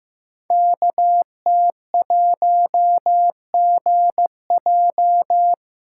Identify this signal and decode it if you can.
Morse code, heard as a keyed tone, about 15 wpm, 700 Hz. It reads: KT1GJ
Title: Code Groups - 5 Letter:Numbers - 15wpm